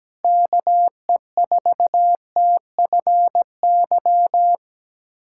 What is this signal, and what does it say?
Morse code, heard as a keyed tone, about 17 wpm, 700 Hz. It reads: KE4TFY
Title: USA Callsign DE Callsign 17wpm